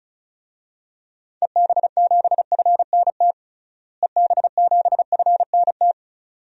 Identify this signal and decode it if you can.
Morse code, heard as a keyed tone, about 34 wpm, 700 Hz. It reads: EB7FNT EB7FNT
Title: DXCC-100 Call Signs - 6 Characters - Repeated 2x - 35wpm